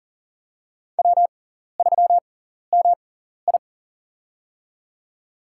Morse code, heard as a keyed tone, about 40 wpm, spaced as for 14 wpm, 700 Hz. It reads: W3MI
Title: USA Call Signs - 4 Characters - 6x Character Spacing - 40wpm